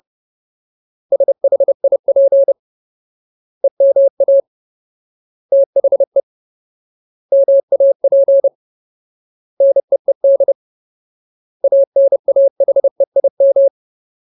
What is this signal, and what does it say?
Morse code, heard as a keyed tone, about 30 wpm, 550 Hz. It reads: SHIP EMA THE MAP NEED ANAHEIM
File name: Ultimate Word List / 30WPM Ultimate